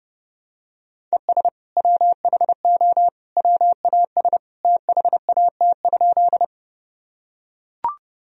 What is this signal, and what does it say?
Morse code, heard as a keyed tone, about 30 wpm, 700 Hz. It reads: ES WHO WAS THAT?
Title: Sentences from Top 100 Words and Abbreviations 30wpm